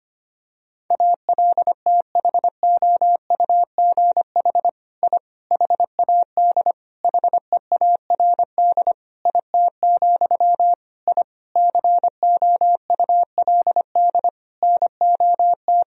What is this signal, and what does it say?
Morse code, heard as a keyed tone, about 25 wpm, 700 Hz. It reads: ALTHOUGH I HAD HEARD IT, I COULD NOT
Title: Sentences from Top 1000 Words - Review Part 3 - 25wpm